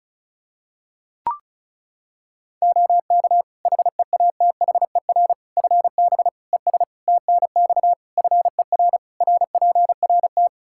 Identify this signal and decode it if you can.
Morse code, heard as a keyed tone, about 36 wpm, 700 Hz. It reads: OK HEATHER FB ES TNX FER RPRT
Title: K1USN SST QSO Practice 35wpm